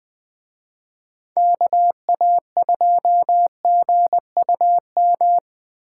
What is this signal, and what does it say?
Morse code, heard as a keyed tone, about 20 wpm, 700 Hz. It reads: KA2GUM
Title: Speed Racing - USA 2x3 Call Signs 17wpm